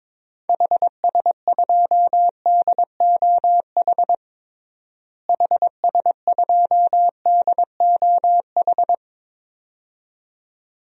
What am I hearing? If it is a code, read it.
Morse code, heard as a keyed tone, about 22 wpm, 700 Hz. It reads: HS2DOH HS2DOH
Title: DXCC-100 Call Signs - 6 Characters - Repeated 2x - 22wpm